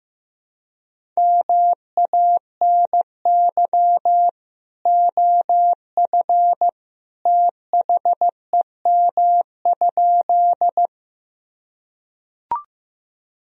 Morse code, heard as a keyed tone, about 15 wpm, 700 Hz. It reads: MANY OF THEM?
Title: Sentences from Top 1000 Words - Review Part 1 - 15wpm